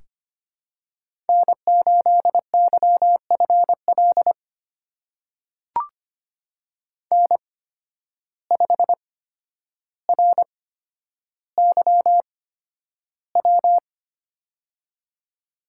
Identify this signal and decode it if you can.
Morse code, heard as a keyed tone, about 25 wpm, spaced as for 11 wpm, 700 Hz. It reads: N8YFL N5RYW
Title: USA Call Signs - 8x Character Spacing - 25wpm